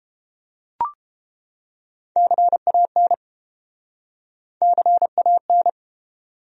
Morse code, heard as a keyed tone, about 31 wpm, 700 Hz. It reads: CAN CAN